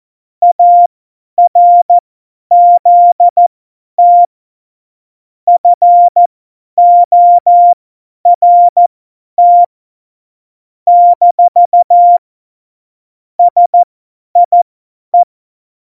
Morse code, heard as a keyed tone, about 14 wpm, spaced as for 10 wpm, 700 Hz. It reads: ARZT FORT - SIE
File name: Morsekurs / Witze / Witz 13 March 2018, 50 F70